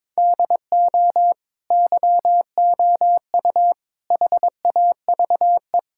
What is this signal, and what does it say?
Morse code, heard as a keyed tone, about 22 wpm, 700 Hz. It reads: DO YOU HAVE
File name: Sets of 9 Words - 22wpm